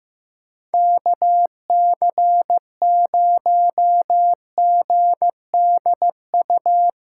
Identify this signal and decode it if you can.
Morse code, heard as a keyed tone, about 16 wpm, 700 Hz. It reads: KC0GDU
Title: Speed Racing - USA 2x3 Call Signs 15wpm